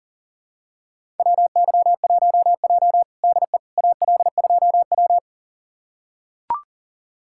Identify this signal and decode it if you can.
Morse code, heard as a keyed tone, about 37 wpm, 700 Hz. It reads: WY1J DE AL2W
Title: USA Callsign DE Callsign - 4 Characters - 40wpm